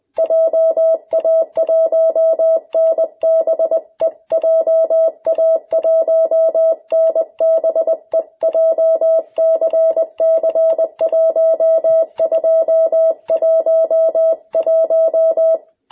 Morse code, heard as a keyed tone, about 20 wpm, 660 Hz. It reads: JA1NBE JA1NBE JCC1211
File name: jh8cbh / 2021cw